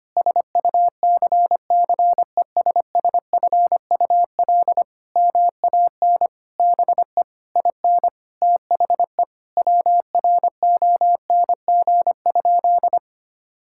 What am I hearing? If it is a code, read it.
Morse code, heard as a keyed tone, about 25 wpm, 700 Hz. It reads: SUCCESSFUL MAN BE IN THE WRONG?
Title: Sentences from Top 1000 Words - Review Part 6 - 25wpm